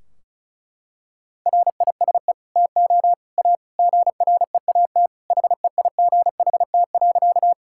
Morse code, heard as a keyed tone, about 34 wpm, 700 Hz. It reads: RISE TO A GREAT HEIGHT.
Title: Sentences from Top 1600 Words 35wpm